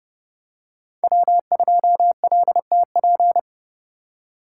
Morse code, heard as a keyed tone, about 28 wpm, 700 Hz. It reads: W2LTP